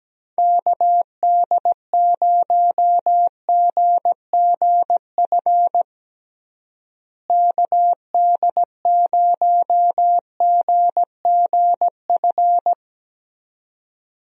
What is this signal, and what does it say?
Morse code, heard as a keyed tone, about 17 wpm, 700 Hz. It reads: KD0GGF KD0GGF
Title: USA Call Signs - 6 Characters - Repeated 2x - 17wpm